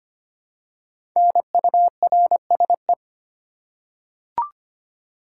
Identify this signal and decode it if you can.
Morse code, heard as a keyed tone, about 25 wpm, 700 Hz. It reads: NURSE
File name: 5 Letter Words - First 3 Letters Given - 25wpm